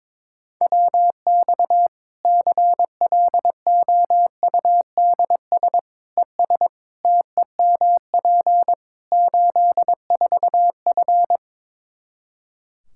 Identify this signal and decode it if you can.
Morse code, heard as a keyed tone, about 22 wpm, 700 Hz. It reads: WX CLOUDS ES TEMP 84F
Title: Practice QSO - Exchange 2 - Info - 22wpm